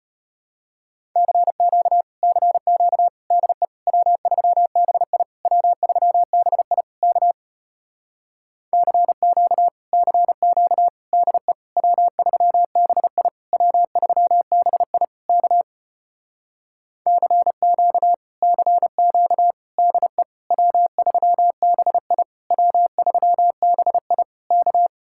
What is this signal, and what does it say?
Morse code, heard as a keyed tone, about 33 wpm, 700 Hz. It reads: CQ CQ DE W3BI W3BI K CQ CQ DE W3BI W3BI K CQ CQ DE W3BI W3BI K